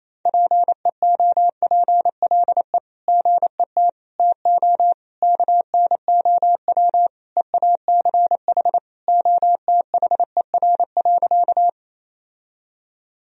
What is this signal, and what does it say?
Morse code, heard as a keyed tone, about 28 wpm, 700 Hz. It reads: PEOPLE GET TO KNOW EACH OTHER.
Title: Sentences from Top 1000 Words - Review Part 1 - 28wpm